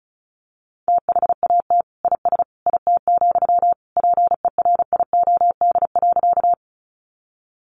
Morse code, heard as a keyed tone, about 34 wpm, 700 Hz. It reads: THAT IS IT, PERIOD.
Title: Sentences from Top 800 Words 35wpm